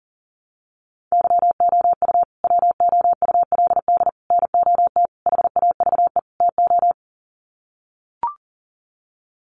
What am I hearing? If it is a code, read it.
Morse code, heard as a keyed tone, about 37 wpm, 700 Hz. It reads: YOU WOULD NOT HAVE TO
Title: Sets of 5 Words Encore 40wpm